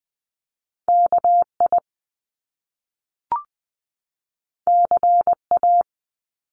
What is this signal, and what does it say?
Morse code, heard as a keyed tone, about 20 wpm, 700 Hz. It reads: KI CA
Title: Top Two Letter Combinations in Words 20wpm